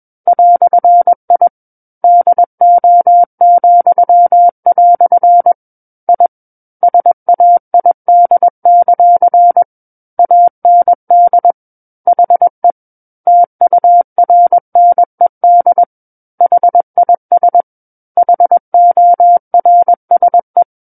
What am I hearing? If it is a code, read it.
Morse code, heard as a keyed tone, about 21 wpm, 700 Hz. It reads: "I DO," I SAID; AND HE TURNED HIS HORSE
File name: Book One / 12-Chapter 12 - 21 WPM